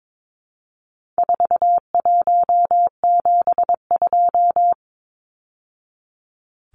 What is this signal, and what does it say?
Morse code, heard as a keyed tone, about 22 wpm, 700 Hz. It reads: 4172
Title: Numbers - 4-Digit 22wpm